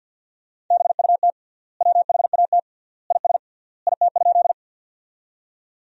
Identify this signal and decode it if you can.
Morse code, heard as a keyed tone, about 50 wpm, 700 Hz. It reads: BUT WHAT IS IT?
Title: 3x Word-Spacing - Sentences from Top 100 Words - 50wpm